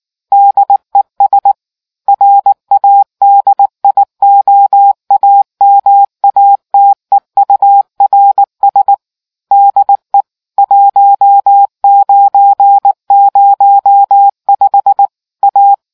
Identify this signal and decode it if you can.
Morse code, heard as a keyed tone, about 19 wpm, 800 Hz. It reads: DES RADIOAMATEURS DE 1905 A